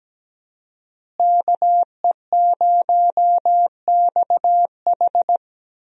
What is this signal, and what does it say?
Morse code, heard as a keyed tone, about 17 wpm, 700 Hz. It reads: KE0XH
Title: USA Call Signs - 5 Characters - Repeated 2x - 17wpm